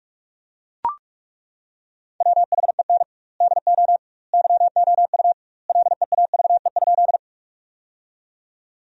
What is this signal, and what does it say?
Morse code, heard as a keyed tone, about 41 wpm, 700 Hz. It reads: WHEN DO YOU LEAVE?
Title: 2x Word-Spacing - Sentences from Top 400 Words - 45wpm